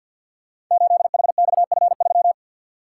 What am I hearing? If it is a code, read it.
Morse code, heard as a keyed tone, about 52 wpm, 700 Hz. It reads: 8HXF3
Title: Code Groups - 5 Letter:Numbers - 50wpm